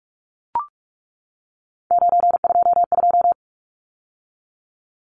Morse code, heard as a keyed tone, about 44 wpm, 700 Hz. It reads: 922
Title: Numbers - 3-Digit 45wpm